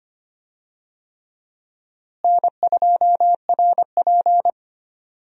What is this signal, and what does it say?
Morse code, heard as a keyed tone, about 25 wpm, 700 Hz. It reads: N2RP
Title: USA Call Signs - 4 Characters - Repeated 3x - 25wpm